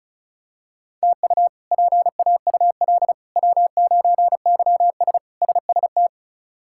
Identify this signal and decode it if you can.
Morse code, heard as a keyed tone, about 34 wpm, 700 Hz. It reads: TU PAUL W9YS SST
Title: K1USN SST QSO Practice - 35wpm